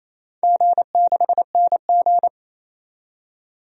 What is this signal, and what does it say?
Morse code, heard as a keyed tone, about 27 wpm, 700 Hz. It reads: G6NG
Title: DXCC-100 Call Signs - 4 Characters - 28wpm